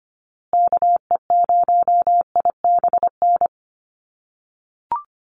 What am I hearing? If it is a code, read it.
Morse code, heard as a keyed tone, about 25 wpm, 700 Hz. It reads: KE0IBN
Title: USA Call Signs Repeated 3x 25wpm